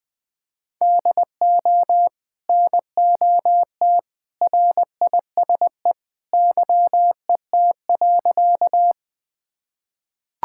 Morse code, heard as a keyed tone, about 20 wpm, 700 Hz. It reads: DO NOT RISE YET.
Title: Sentences from Top 1000 Words - Review Part 6 - 20wpm